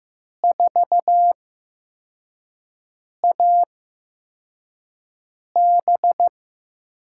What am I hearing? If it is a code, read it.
Morse code, heard as a keyed tone, about 15 wpm, 700 Hz. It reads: 4 A B
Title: USA Call Signs - 5 Characters - 8x Character Spacing - 15wpm